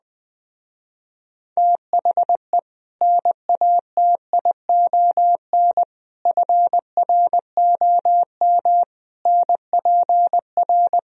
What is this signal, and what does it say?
Morse code, heard as a keyed tone, about 20 wpm, 700 Hz. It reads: THE NATION FROM NPR